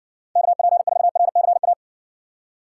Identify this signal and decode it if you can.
Morse code, heard as a keyed tone, about 70 wpm, 700 Hz. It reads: KP4RCA